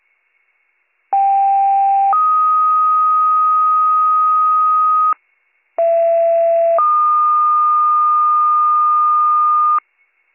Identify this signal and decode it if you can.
Two-tone page, heard as a one-second tone, then a three-second tone, about 790 then 1250 Hz, about 670 then 1220 Hz. It is heard from 1.1 to 9.8 s.